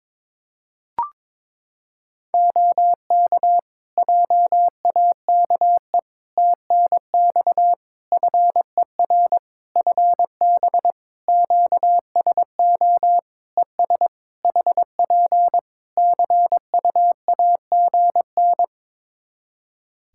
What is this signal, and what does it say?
Morse code, heard as a keyed tone, about 22 wpm, 700 Hz. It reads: OK JAKE TNX FER FB QSO ES HP CUAGN
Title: Practice QSO - Exchange 4 - Ending - 22wpm